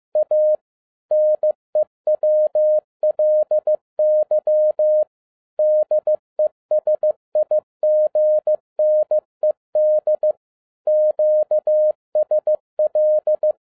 Morse code, heard as a keyed tone, about 15 wpm, 610 Hz. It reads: A NEWLY DESIGNED QSL
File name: 2012-01-mp3 / n13-A NEWLY DESIGNED QSL CARD-15wpm